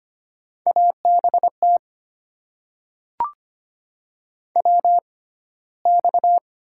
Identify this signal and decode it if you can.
Morse code, heard as a keyed tone, about 25 wpm, spaced as for 15 wpm, 700 Hz. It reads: ABT WX
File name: Top 90 QSO Elements - 6x Character Spacing - 25wpm